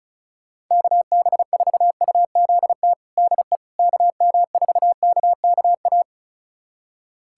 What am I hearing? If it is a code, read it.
Morse code, heard as a keyed tone, about 34 wpm, 700 Hz. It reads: KB4UZT DE KM4KKA
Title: USA Callsign DE Callsign - 6 Characters - 35wpm